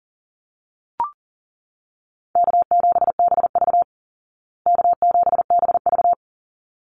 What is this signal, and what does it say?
Morse code, heard as a keyed tone, about 40 wpm, 700 Hz. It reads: K7BV K7BV
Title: Easy Call Signs Repeated 2x - 40wpm